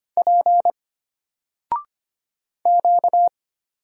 Morse code, heard as a keyed tone, about 25 wpm, 700 Hz. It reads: P Q
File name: Single-Letter-Number-25wpm